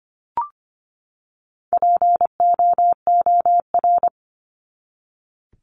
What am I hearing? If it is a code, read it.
Morse code, heard as a keyed tone, about 25 wpm, 700 Hz. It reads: POOR
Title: Top 751-1000 Words 25wpm